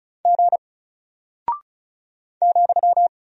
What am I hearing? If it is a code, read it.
Morse code, heard as a keyed tone, about 34 wpm, 700 Hz. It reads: G ,
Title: Single Character (Letter:Number:Punctuation:Prosign) Alt 35wpm